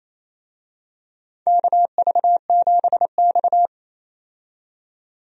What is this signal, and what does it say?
Morse code, heard as a keyed tone, about 28 wpm, 700 Hz. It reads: KV7X
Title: USA Call Signs - 4 Characters - 28wpm